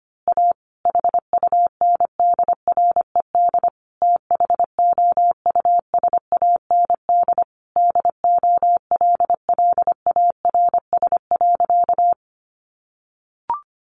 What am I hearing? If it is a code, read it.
Morse code, heard as a keyed tone, about 25 wpm, 700 Hz. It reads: A HUNDRED THOUSAND DOLLARS.
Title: A Sound Of Thunder 25wpm